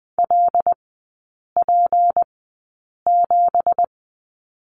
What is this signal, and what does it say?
Morse code, heard as a keyed tone, about 20 wpm, spaced as for 8 wpm, 700 Hz. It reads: LP7
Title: 166 Lesson 27e - Number 7 - Three Characters 20wpm